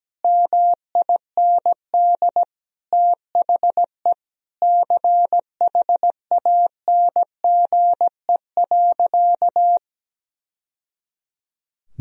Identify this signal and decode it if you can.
Morse code, heard as a keyed tone, about 17 wpm, 700 Hz. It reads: MIND THE CHANGE.
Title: Sentences from Top 1000 Words - Review Part 1 - 17wpm